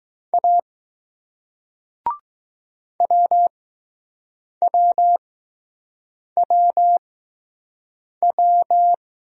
Morse code, heard as a keyed tone, about 20 wpm, spaced as for 6 wpm, 700 Hz. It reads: A WWWW